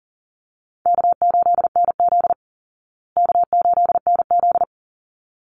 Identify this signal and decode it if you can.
Morse code, heard as a keyed tone, about 40 wpm, 700 Hz. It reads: K8NZ K8NZ
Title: Call Signs Repeated 3x - 40wpm